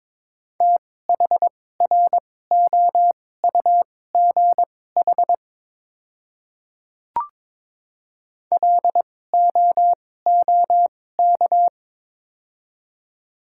Morse code, heard as a keyed tone, about 22 wpm, spaced as for 16 wpm, 700 Hz. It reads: THROUGH LOOK
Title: 2x Character Spacing - Top 200 Words - 22wpm